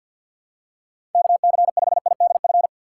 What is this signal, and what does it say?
Morse code, heard as a keyed tone, about 52 wpm, 700 Hz. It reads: KC5IDF
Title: USA Call Signs - 3x Character Spacing - 50wpm